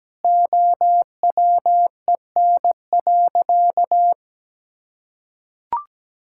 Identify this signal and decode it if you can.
Morse code, heard as a keyed tone, about 17 wpm, 700 Hz. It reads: OWEN.
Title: Pangram Challenge v2 - 17wpm